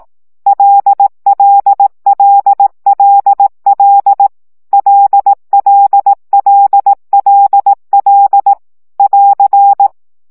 Morse code, heard as a keyed tone, about 18 wpm, 800 Hz. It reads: LLLLL LLLLL <AR>
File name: CW gyakorlas l 15 baud